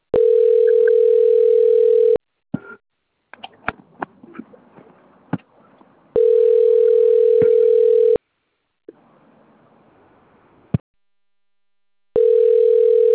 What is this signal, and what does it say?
Ringback tone (North America), 440 + 480 Hz.
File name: rg-1600-unknown- - - .42887